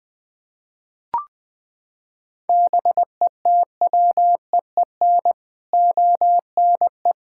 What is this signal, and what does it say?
Morse code, heard as a keyed tone, about 19 wpm, 700 Hz. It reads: BETWEEN ONE
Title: Sets of 6 Words - 20wpm